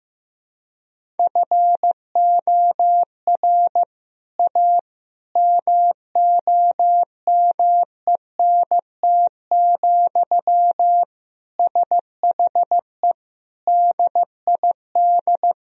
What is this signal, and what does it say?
Morse code, heard as a keyed tone, about 15 wpm, 700 Hz. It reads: FOR A MOMENT, SHE DID
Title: Sentences from Top 1000 Words - Review Part 2 - 15wpm